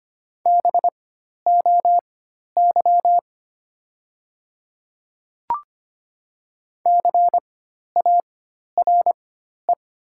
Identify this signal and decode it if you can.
Morse code, heard as a keyed tone, about 25 wpm, spaced as for 12 wpm, 700 Hz. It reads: BOY CARE
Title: 4x Character Spacing - Top 300 Words - 25wpm